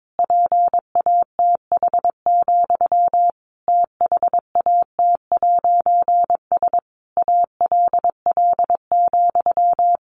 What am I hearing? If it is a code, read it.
Morse code, heard as a keyed tone, about 22 wpm, 700 Hz. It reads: PATH, THAT'S ALL,
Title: A Sound Of Thunder 22wpm